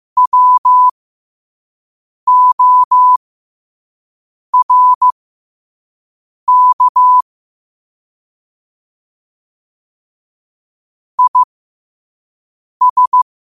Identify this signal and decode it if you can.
Morse code, heard as a keyed tone, about 15 wpm, spaced as for 5 wpm, 1000 Hz. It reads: WORK IS
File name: qotdfiles / 5wpm